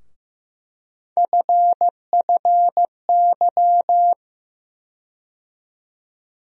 Morse code, heard as a keyed tone, about 15 wpm, 700 Hz. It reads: FFY
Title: USA Call Sign Postfixes - 3 Letters 15wpm